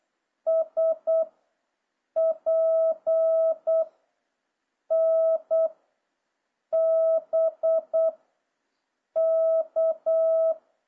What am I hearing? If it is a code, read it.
Morse code, heard as a keyed tone, about 8 wpm, 650 Hz. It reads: S P N B K